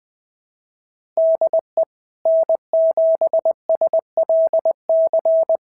Morse code, heard as a keyed tone, about 20 wpm, 690 Hz. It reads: DE N7SLC
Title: SOTA QSO Practice 20wpm